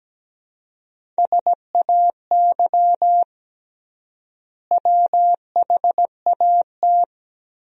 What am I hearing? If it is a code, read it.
Morse code, heard as a keyed tone, about 17 wpm, 700 Hz. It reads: SAY WHAT
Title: 3x Word-Spacing - Sentences from Top 200 Words - 17wpm